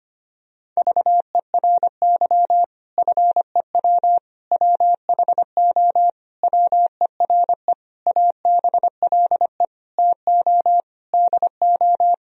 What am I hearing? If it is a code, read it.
Morse code, heard as a keyed tone, about 25 wpm, 700 Hz. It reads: VERY FEW WHO WERE ABLE TO DO
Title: Sentences from Top 1000 Words - Review Part 2 - 25wpm